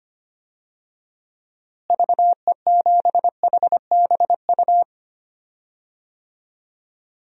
Morse code, heard as a keyed tone, about 25 wpm, 700 Hz. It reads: VE7HBU
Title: DXCC-100 Call Signs - 6 Characters - Repeated 2x - 25wpm